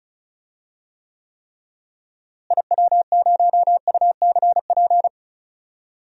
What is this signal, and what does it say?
Morse code, heard as a keyed tone, about 36 wpm, 700 Hz. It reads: IW0UCP
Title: DXCC-100 Call Signs - 6 Characters - Repeated 3x - 35wpm